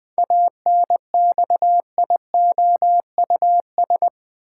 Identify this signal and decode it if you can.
Morse code, heard as a keyed tone, about 21 wpm, 700 Hz. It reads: ANXIOUS
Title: Seven Letter Words - 20wpm